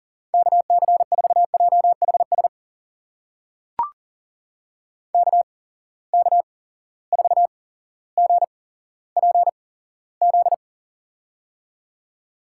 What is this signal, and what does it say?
Morse code, heard as a keyed tone, about 40 wpm, spaced as for 17 wpm, 700 Hz. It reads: KC4JHS KK4GPZ
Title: USA Call Signs - 6 Characters - 8x Character Spacing - 40wpm